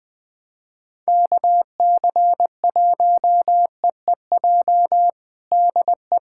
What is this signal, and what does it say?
Morse code, heard as a keyed tone, about 21 wpm, 700 Hz. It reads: KC1EEJ DE
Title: USA Callsign DE Callsign - 6 Characters - 20wpm